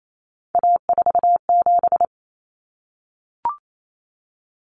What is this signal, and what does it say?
Morse code, heard as a keyed tone, about 28 wpm, 700 Hz. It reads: A47
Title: Common Cut Numbers - 3-Digit 28wpm